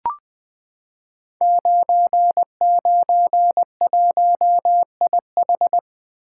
Morse code, heard as a keyed tone, about 20 wpm, 700 Hz. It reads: 991IH
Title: Code Groups - 5 Letter:Numbers - 20wpm